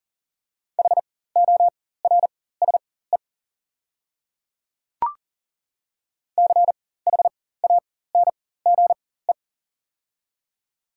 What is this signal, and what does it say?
Morse code, heard as a keyed tone, about 40 wpm, spaced as for 19 wpm, 690 Hz. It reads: HORSE CHANGE
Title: 4x Character Spacing - Top 500 Words - 40wpm